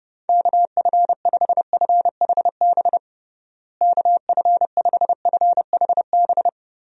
Morse code, heard as a keyed tone, about 29 wpm, 700 Hz. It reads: KF5FHB KF5FHB
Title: USA Call Signs - 6 Characters - Repeated 3x - 30wpm